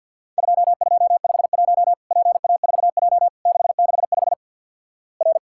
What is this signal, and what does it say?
Morse code, heard as a keyed tone, about 50 wpm, 700 Hz. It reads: 2251 PA4J 665 R
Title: Worked All Europe DX Contest - QTC Exchange - 50wpm